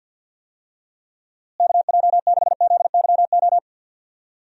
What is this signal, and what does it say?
Morse code, heard as a keyed tone, about 48 wpm, 700 Hz. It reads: KJ6ZYO
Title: USA Call Signs - 6 Characters - Repeated 3x - 50wpm